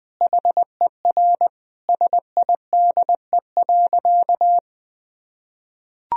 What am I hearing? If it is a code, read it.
Morse code, heard as a keyed tone, about 21 wpm, 700 Hz. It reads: HER SIDE.
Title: Sentences from Top 1000 Words - Review Part 1 - 20wpm